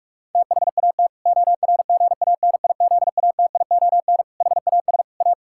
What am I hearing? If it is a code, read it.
Morse code, heard as a keyed tone, about 41 wpm, 700 Hz. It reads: THAT ORGANIZATION HAS A
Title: Sentences from Top 1000 Words - Review Part 6 - 45wpm